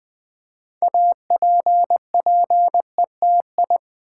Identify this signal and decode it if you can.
Morse code, heard as a keyed tone, about 20 wpm, 700 Hz. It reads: APPETI
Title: Build A Word - 2 Letters At A Time - 8-Letter Words - 20wpm